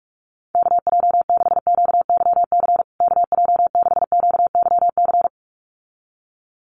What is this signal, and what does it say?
Morse code, heard as a keyed tone, about 44 wpm, 700 Hz. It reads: KJ6QYC KJ6QYC
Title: USA Call Signs Repeated 2x 45wpm